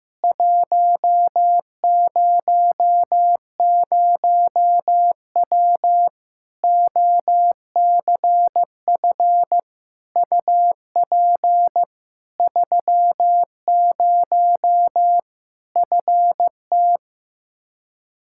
Morse code, heard as a keyed tone, about 15 wpm, 700 Hz. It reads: 100W OCF UP 30 FT
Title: Practice QSO - Exchange 2 - Info - 15wpm